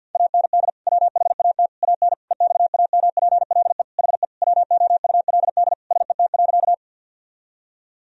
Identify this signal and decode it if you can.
Morse code, heard as a keyed tone, about 50 wpm, 700 Hz. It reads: AND WHAT AN EXAMPLE HE WOULD SET.
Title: Sentences from Top 1000 Words - Review Part 1 - 50wpm